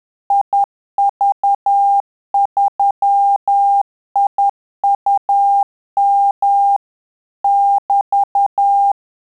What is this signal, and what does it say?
Morse code, heard as a keyed tone, about 10 wpm, 800 Hz. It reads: IV3IUM <BT>
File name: Telegraphy-group / QSO CW 10 PPM